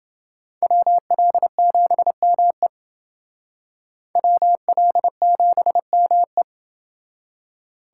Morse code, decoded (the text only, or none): WL7ME WL7ME